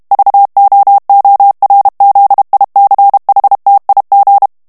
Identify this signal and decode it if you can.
Morse code, heard as a keyed tone, about 32 wpm, 800 Hz. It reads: VOORZICHTIG
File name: HOLLANDAIS texte clair 2 vitesse 30 mots minute